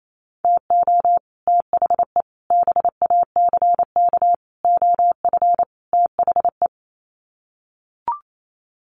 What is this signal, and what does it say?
Morse code, heard as a keyed tone, about 28 wpm, 700 Hz. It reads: TO THE BACK OF THE
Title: Sets of 5 Words 28wpm